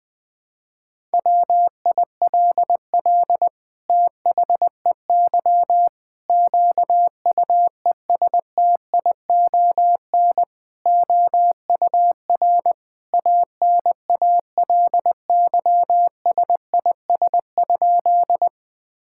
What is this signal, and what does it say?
Morse code, heard as a keyed tone, about 20 wpm, 700 Hz. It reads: WILL THEY QUESTION OUR ANALYSIS?
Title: Sentences from Top 1000 Words - Review Part 5 - 20wpm